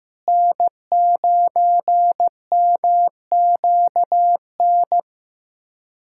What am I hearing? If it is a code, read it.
Morse code, heard as a keyed tone, about 15 wpm, 700 Hz. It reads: N9MQN
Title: USA Callsign DE Callsign - 5 Characters - 15wpm